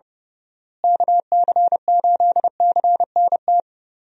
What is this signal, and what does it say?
Morse code, heard as a keyed tone, about 32 wpm, 700 Hz. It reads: KC8CNT
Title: USA Call Signs - 6 Characters - Repeated 2x - 30wpm